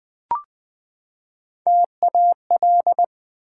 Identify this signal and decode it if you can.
Morse code, heard as a keyed tone, about 20 wpm, 700 Hz. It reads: TAL
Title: Build A Word - 3 Letters At A Time - 6-Letter Words - 20wpm